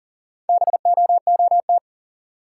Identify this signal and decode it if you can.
Morse code, heard as a keyed tone, about 40 wpm, 700 Hz. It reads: BOOT
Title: Four Letter Words - 40wpm